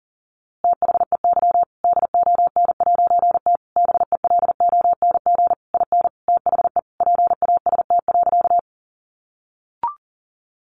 Morse code, heard as a keyed tone, about 40 wpm, 700 Hz. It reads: THEY DON'T BELONG IN THE PAST.
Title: A Sound Of Thunder 40wpm